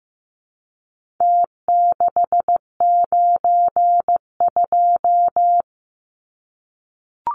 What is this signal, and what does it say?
Morse code, heard as a keyed tone, about 15 wpm, 700 Hz. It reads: T692
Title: Numbers - Contest Style - 4-Digit 15wpm